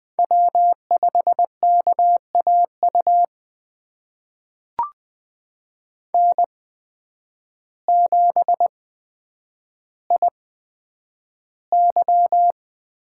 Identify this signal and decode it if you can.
Morse code, heard as a keyed tone, about 20 wpm, spaced as for 9 wpm, 700 Hz. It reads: W5KAU N7IY
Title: USA Call Signs - 8x Character Spacing - 20wpm